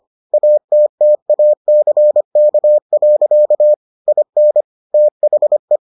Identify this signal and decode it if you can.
Morse code, heard as a keyed tone, about 25 wpm, 600 Hz. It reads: ATTACK. IN THE